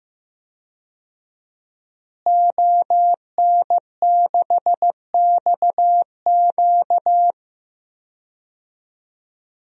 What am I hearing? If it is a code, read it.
Morse code, heard as a keyed tone, about 15 wpm, 700 Hz. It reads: ON6XQ
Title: DXCC-100 Call Signs - 5 Characters - Repeated 2x - 15wpm